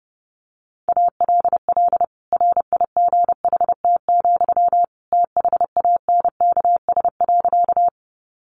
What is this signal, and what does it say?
Morse code, heard as a keyed tone, about 30 wpm, 700 Hz. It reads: ALL RIGHT, THANKS.
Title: Sentences from Top 600 Words 30wpm